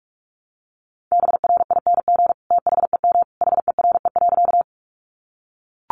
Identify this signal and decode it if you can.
Morse code, heard as a keyed tone, about 44 wpm, 700 Hz. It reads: BRING THEM HERE.
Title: Sentences from Top 500 Words 45wpm